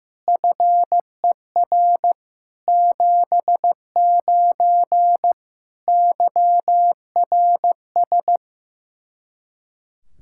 Morse code, heard as a keyed tone, about 15 wpm, 700 Hz. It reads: FER 79 YRS
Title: Practice QSO - Exchange 3 - More Info - 15wpm